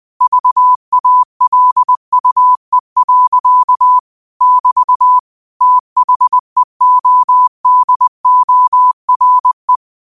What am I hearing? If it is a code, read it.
Morse code, heard as a keyed tone, about 20 wpm, 1000 Hz. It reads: VALUE. <BT> THEODORE
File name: qotdfiles / 20wpm